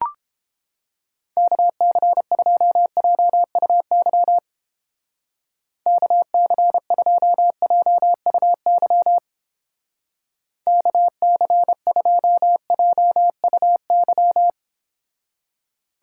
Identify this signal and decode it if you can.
Morse code, heard as a keyed tone, about 31 wpm, 700 Hz. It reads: KC2JUY KC2JUY KC2JUY